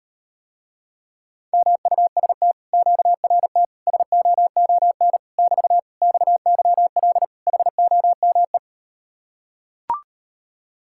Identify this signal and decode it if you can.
Morse code, heard as a keyed tone, about 38 wpm, 700 Hz. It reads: MUST QRT SOON <BT> XYL HOME